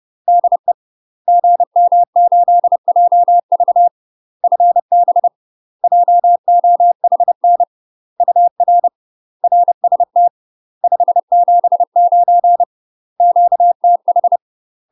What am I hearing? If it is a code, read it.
Morse code, heard as a keyed tone, about 30 wpm, 700 Hz. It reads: DE GM8JV FB JOHN UR RST 579 QTH
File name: UK MASTER QSOs TRAINING/30WPM